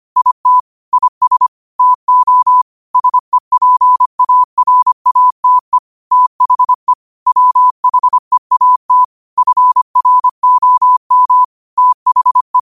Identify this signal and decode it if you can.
Morse code, heard as a keyed tone, about 25 wpm, 1000 Hz. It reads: IT IS TO SEPARATE THE WHEAT FROM THE
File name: qotdfiles / 25wpm